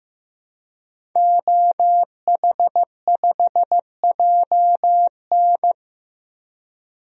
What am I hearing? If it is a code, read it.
Morse code, heard as a keyed tone, about 15 wpm, 700 Hz. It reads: OH5JN
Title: DXCC-100 Call Signs - 5 Characters - 15wpm